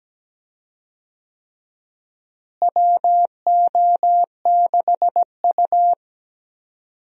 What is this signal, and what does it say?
Morse code, heard as a keyed tone, about 17 wpm, 700 Hz. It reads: WO6U
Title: USA Call Signs - 4 Characters - Repeated 3x - 17wpm